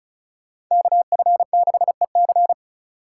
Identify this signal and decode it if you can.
Morse code, heard as a keyed tone, about 42 wpm, 700 Hz. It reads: KF6EC
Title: Speed Racing - USA Call Signs 35wpm